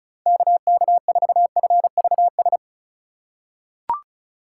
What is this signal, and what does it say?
Morse code, heard as a keyed tone, about 36 wpm, 700 Hz. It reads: KK4FVS
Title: USA Call Signs - 3x Character Spacing - 35wpm